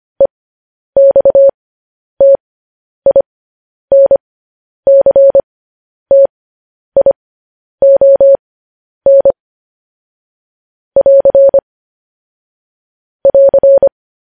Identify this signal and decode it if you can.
Morse code, heard as a keyed tone, about 25 wpm, spaced as for 10 wpm, 550 Hz. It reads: EXTINCTION <AR> <AR>
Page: news headlines in morse code at 10 wpm